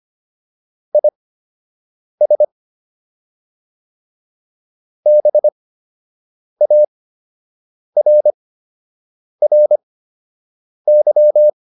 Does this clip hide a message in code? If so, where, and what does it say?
Yes: Morse code from 0.9 to 11.5 s, heard as a keyed tone, about 25 wpm, spaced as for 7 wpm, 600 Hz. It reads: IS BARRY